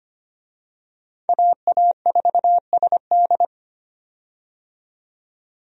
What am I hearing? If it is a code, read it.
Morse code, heard as a keyed tone, about 25 wpm, 700 Hz. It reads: AA4SD
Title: USA Call Signs - 5 Characters - 25wpm